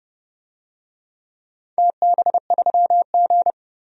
Morse code, heard as a keyed tone, about 30 wpm, 700 Hz. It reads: TB3G
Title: DXCC-100 Call Signs - 4 Characters - 30wpm